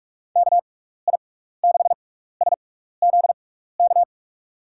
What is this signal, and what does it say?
Morse code, heard as a keyed tone, about 44 wpm, 700 Hz. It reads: K I 6 S Z K
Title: USA Call Signs - 6 Characters - 6x Character Spacing - 45wpm